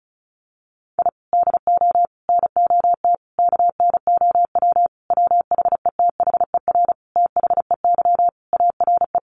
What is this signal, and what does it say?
Morse code, heard as a keyed tone, about 34 wpm, 700 Hz. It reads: I DO NOT KNOW WHETHER THEY ARE
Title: Sentences from Top 1500 Words 35wpm